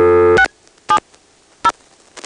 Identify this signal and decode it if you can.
DTMF, heard as telephone touch tones, tones of about 90 ms. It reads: 670
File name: phonetrips / PrinceWmCoVA5HQ